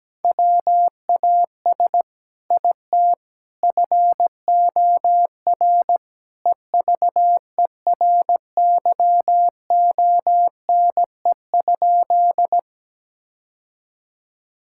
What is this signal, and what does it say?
Morse code, heard as a keyed tone, about 17 wpm, 700 Hz. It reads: WAS IT FOR EVERYONE?
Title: Sentences from Top 1000 Words - Review Part 5 - 17wpm